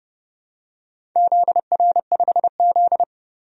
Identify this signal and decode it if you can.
Morse code, heard as a keyed tone, about 30 wpm, 700 Hz. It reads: ZR5Z
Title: DXCC-100 Call Signs - 4 Characters - 30wpm